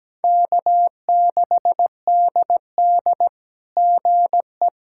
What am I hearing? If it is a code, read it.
Morse code, heard as a keyed tone, about 17 wpm, 700 Hz. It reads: K6DD GE
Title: SOTA QSO Practice 17wpm